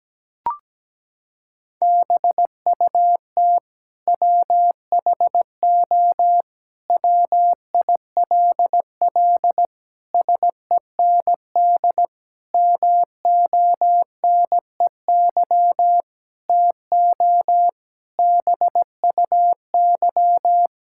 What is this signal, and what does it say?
Morse code, heard as a keyed tone, about 17 wpm, 700 Hz. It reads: BUT WHO WILL SEND MONEY TO BUY
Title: Sentences from Top 1600 Words 17wpm